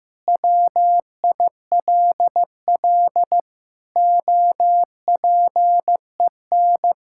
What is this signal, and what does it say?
Morse code, heard as a keyed tone, about 15 wpm, 700 Hz. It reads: WILL OPEN
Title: Sentences from Top 1600 Words 15wpm